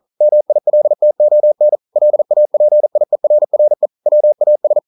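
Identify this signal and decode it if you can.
Morse code, heard as a keyed tone, about 40 wpm, 600 Hz. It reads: MILTON LAPIERRE WAS
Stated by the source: Morse code practice at 41 wpm, from CW med Gnister.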